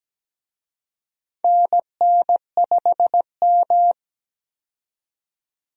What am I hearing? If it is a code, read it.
Morse code, heard as a keyed tone, about 17 wpm, 700 Hz. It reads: NN5M
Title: USA Call Signs - 4 Characters - 17wpm